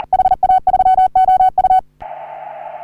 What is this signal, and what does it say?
Morse code, heard as a keyed tone, about 40 wpm, 750 Hz. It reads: HA3OU
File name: CW / HA3OU RHOBCW REC